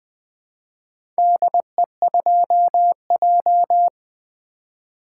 Morse code, heard as a keyed tone, about 20 wpm, 700 Hz. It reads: DE2J
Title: Four Letter-Number Combinations - 20wpm